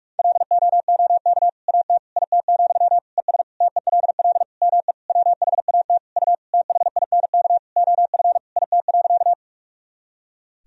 Morse code, heard as a keyed tone, about 46 wpm, 700 Hz. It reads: LOOK AT IT, ES TELL ME WHAT U THINK OF IT.
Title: Sentences from Top 200 Words and Abbreviations 45wpm